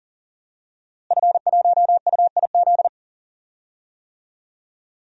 Morse code, heard as a keyed tone, about 37 wpm, 700 Hz. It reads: F1UIZ
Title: DXCC-100 Call Signs - 5 Characters - Repeated 2x - 40wpm